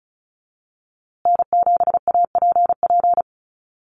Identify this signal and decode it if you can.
Morse code, heard as a keyed tone, about 34 wpm, 700 Hz. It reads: N7APP
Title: USA Call Signs Repeated 2x 35wpm